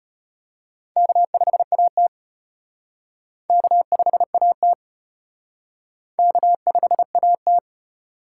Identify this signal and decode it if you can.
Morse code, heard as a keyed tone, about 32 wpm, 700 Hz. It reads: K5AT K5AT K5AT